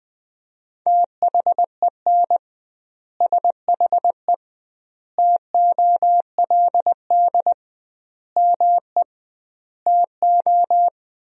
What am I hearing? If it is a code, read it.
Morse code, heard as a keyed tone, about 20 wpm, 700 Hz. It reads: THEN SHE TOLD ME TO
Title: 2x Word-Spacing - Sentences from Top 400 Words - 20wpm